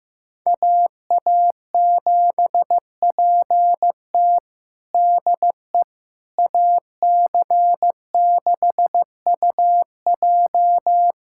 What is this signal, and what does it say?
Morse code, heard as a keyed tone, about 16 wpm, 700 Hz. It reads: AA7PT DE AC6UJ
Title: USA Callsign DE Callsign - 5 Characters - 15wpm